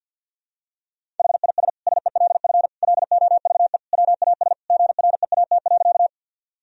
Morse code, heard as a keyed tone, about 48 wpm, 700 Hz. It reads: HIS SELF LOVE WAS GREAT.
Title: Sentences from Top 1000 Words - Review Part 1 - 50wpm